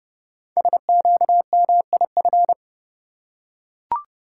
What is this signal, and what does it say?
Morse code, heard as a keyed tone, about 30 wpm, 700 Hz. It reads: SQMIF
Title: Code Groups - 5 Letters - 30wpm